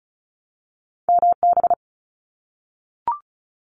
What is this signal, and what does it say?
Morse code, heard as a keyed tone, about 36 wpm, 700 Hz. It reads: MB
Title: Canadian Provinces and Territories 35wpm